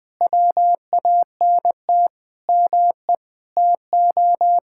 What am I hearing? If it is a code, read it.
Morse code, heard as a keyed tone, about 20 wpm, 700 Hz. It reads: WANT ME TO
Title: Sentences with 4-Word Phrases - 20wpm